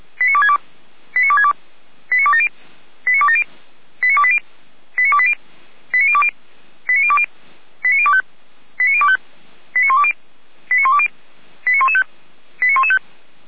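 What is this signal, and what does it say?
Five-tone selective call (ZVEI), heard as quick runs of five tones, tones of about 70 ms. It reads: 89262 89262 89270 89270 89280 89280 89020 89020 89025 89025 89120 89120 89115 89115
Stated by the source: ZVEI five-tone selective call 89358 received via SDR.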